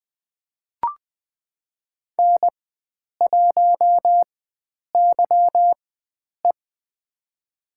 Morse code, heard as a keyed tone, about 20 wpm, spaced as for 9 wpm, 700 Hz. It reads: N1YE
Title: USA Call Signs - 4 Characters - 4x Character Spacing - 20wpm